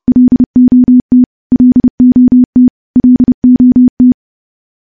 Morse code, heard as a keyed tone, about 30 wpm, 250 Hz. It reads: LOT LOT LOT
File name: morse / morse-training-30WPM